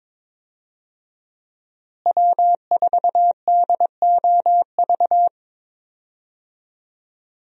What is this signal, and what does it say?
Morse code, heard as a keyed tone, about 22 wpm, 700 Hz. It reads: W4DOV
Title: USA Call Signs - 5 Characters - Repeated 3x - 22wpm